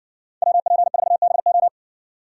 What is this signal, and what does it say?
Morse code, heard as a keyed tone, about 57 wpm, 700 Hz. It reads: WP4BO